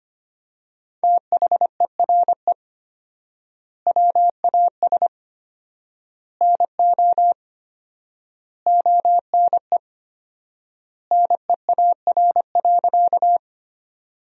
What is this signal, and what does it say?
Morse code, heard as a keyed tone, about 25 wpm, 700 Hz. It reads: THERE WAS NO ONE NEAR.
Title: 4x Word-Spacing - Sentences from Top 200 Words - 25wpm